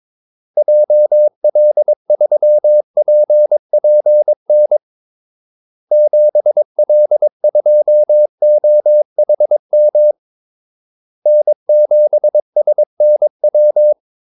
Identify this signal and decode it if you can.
Morse code, heard as a keyed tone, about 22 wpm, 600 Hz. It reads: JL3PPN 7L2OHM N7SNW
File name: Callsigns-22wpm